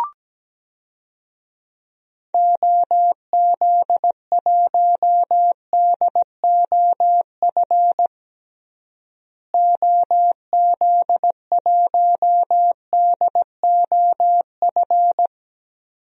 Morse code, heard as a keyed tone, about 17 wpm, 700 Hz. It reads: OZ1DOF OZ1DOF
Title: DXCC-100 Call Signs - 6 Characters - Repeated 2x - 17wpm